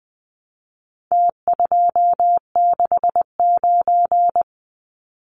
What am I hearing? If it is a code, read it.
Morse code, heard as a keyed tone, about 20 wpm, 700 Hz. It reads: T269
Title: Numbers - Contest Style - 4-Digit 20wpm